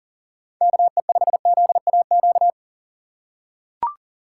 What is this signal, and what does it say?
Morse code, heard as a keyed tone, about 40 wpm, 700 Hz. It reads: KE5ZAQ
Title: USA Call Signs - 2x Character Spacing - 40wpm